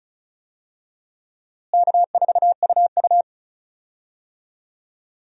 Morse code, heard as a keyed tone, about 36 wpm, 700 Hz. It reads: K4UU
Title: USA Call Signs - 4 Characters - Repeated 2x - 35wpm